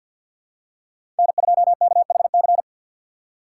Morse code, heard as a keyed tone, about 50 wpm, 700 Hz. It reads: N2KHC
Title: USA Call Signs - 5 Characters - 2x Character Spacing - 50wpm